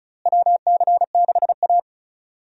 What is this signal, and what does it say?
Morse code, heard as a keyed tone, about 34 wpm, 700 Hz. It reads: WC6A
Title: USA Call Signs - 4 Characters - Repeated 3x - 35wpm